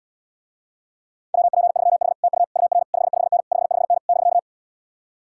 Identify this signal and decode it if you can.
Morse code, heard as a keyed tone, about 78 wpm, 700 Hz. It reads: WP3H TU UR 55N 55N <BK>